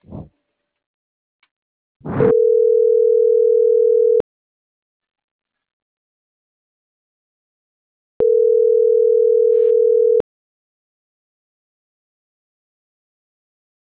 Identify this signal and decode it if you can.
Ringback tone (North America), 440 + 480 Hz.